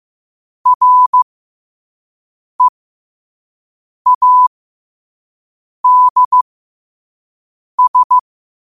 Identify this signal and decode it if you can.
Morse code, heard as a keyed tone, about 15 wpm, spaced as for 5 wpm, 1000 Hz. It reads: READS
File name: qotdfiles / 5wpm